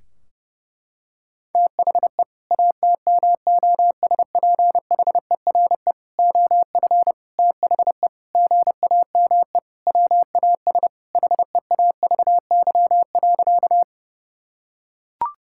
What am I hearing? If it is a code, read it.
Morse code, heard as a keyed tone, about 30 wpm, 700 Hz. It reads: THE ATMOSPHERE OF THE GAME WAS HEAVY.
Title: Sentences from Top 1600 Words 30wpm